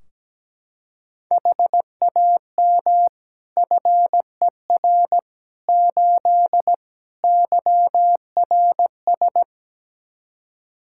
Morse code, heard as a keyed tone, about 17 wpm, 700 Hz. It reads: HAM FER 8 YRS
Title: Practice QSO - Exchange 3 - More Info - 17wpm